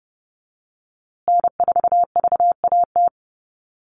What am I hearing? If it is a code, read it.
Morse code, heard as a keyed tone, about 30 wpm, 700 Hz. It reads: N4VAT
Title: USA Call Signs Repeated 3x 30wpm